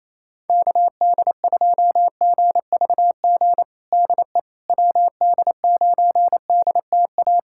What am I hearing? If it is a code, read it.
Morse code, heard as a keyed tone, about 28 wpm, 700 Hz. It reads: KD2GVG DE WD9DTA
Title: USA Callsign DE Callsign - 6 Characters - 28wpm